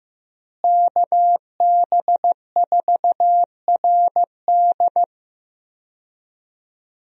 Morse code, heard as a keyed tone, about 14 wpm, 700 Hz. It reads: KB4RD
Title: USA Call Signs - 5 Characters - Repeated 2x - 15wpm